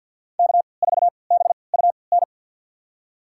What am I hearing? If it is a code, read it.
Morse code, heard as a keyed tone, about 51 wpm, spaced as for 29 wpm, 700 Hz. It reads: K4BUN
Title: USA Call Signs - 3x Character Spacing - 50wpm